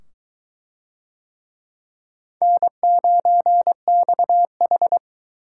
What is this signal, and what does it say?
Morse code, heard as a keyed tone, about 21 wpm, 700 Hz. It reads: N9XH